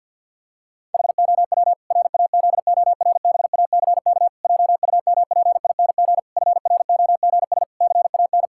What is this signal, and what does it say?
Morse code, heard as a keyed tone, about 50 wpm, 700 Hz. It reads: HOW RAZORBACK JUMPING FROGS CAN
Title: Pangram Challenge v2 - 50wpm